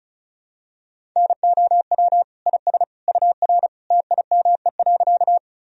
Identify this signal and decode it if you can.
Morse code, heard as a keyed tone, about 34 wpm, 700 Hz. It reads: NOW IS UR TIME.
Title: Sentences from Top 100 Words and Abbreviations 35wpm